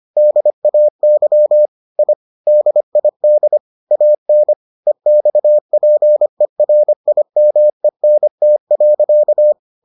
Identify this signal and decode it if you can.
Morse code, heard as a keyed tone, about 25 wpm, 600 Hz. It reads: DAY I DID AN EXPERIMENT.